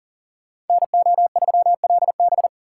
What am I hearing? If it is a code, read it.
Morse code, heard as a keyed tone, about 50 wpm, 700 Hz. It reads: NO3LB
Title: Speed Racing - USA Call Signs 40wpm